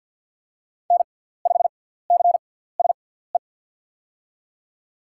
Morse code, heard as a keyed tone, about 47 wpm, spaced as for 17 wpm, 700 Hz. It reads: N5CSE
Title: USA Call Signs - 5 Characters - 6x Character Spacing - 50wpm